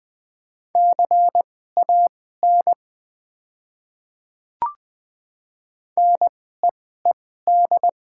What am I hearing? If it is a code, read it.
Morse code, heard as a keyed tone, about 20 wpm, spaced as for 14 wpm, 690 Hz. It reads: CAN NEED
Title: 2x Character Spacing - Top 200 Words - 20wpm